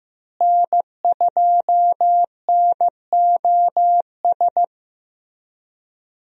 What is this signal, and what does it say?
Morse code, heard as a keyed tone, about 15 wpm, 700 Hz. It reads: N2NOS
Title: USA Call Signs - 5 Characters - Repeated 3x - 15wpm